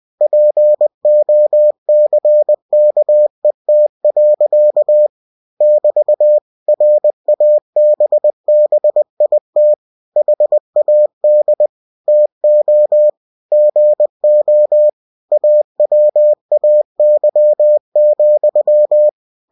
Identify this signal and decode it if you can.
Morse code, heard as a keyed tone, about 20 wpm, 600 Hz. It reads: POCKET. <BT> RABBIT HAD TO GO AWAY,
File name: cw-books / winnie-the-pooh / winnie-20wpm-page-0070